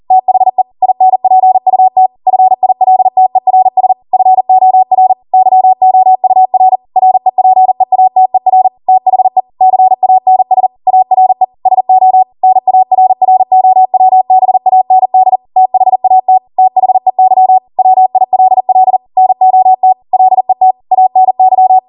Morse code, heard as a keyed tone, about 40 wpm, 750 Hz. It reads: THE INPUT FILTERS FOR YOUR REPEATER THE CANS ARE SO NARROWBAND THAT THEY WILL NOT LET ANY
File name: Morse / 40WPM